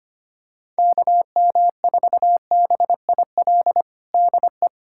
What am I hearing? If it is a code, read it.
Morse code, heard as a keyed tone, about 25 wpm, 700 Hz. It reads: KM4BIL DE
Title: USA Callsign DE Callsign - 6 Characters - 25wpm